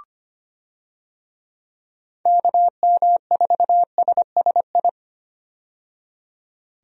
Morse code, heard as a keyed tone, about 25 wpm, 700 Hz. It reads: KM4SSI
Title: USA Call Signs - 6 Characters - 25wpm